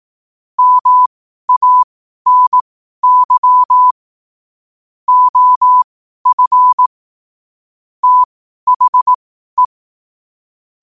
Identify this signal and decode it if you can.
Morse code, heard as a keyed tone, about 18 wpm, spaced as for 12 wpm, 1000 Hz. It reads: MANY OF THE
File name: qotdfiles / 12wpm